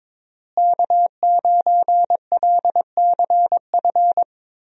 Morse code, heard as a keyed tone, about 22 wpm, 700 Hz. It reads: K9LCF
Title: USA Call Signs - 5 Characters - Repeated 2x - 22wpm